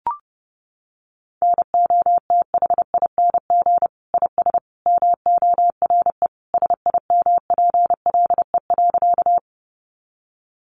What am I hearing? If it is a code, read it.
Morse code, heard as a keyed tone, about 30 wpm, 700 Hz. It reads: NOTHING IS MORE SIMPLE.
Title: Sentences from Top 500 Words 30wpm